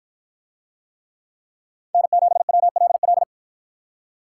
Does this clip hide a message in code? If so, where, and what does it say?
Yes: Morse code from 1.9 to 3.2 s, heard as a keyed tone, about 50 wpm, 700 Hz. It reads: N7WLL